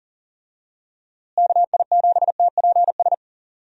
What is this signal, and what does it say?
Morse code, heard as a keyed tone, about 40 wpm, 700 Hz. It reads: KI7TPS
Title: USA Call Signs - 6 Characters - Repeated 2x - 40wpm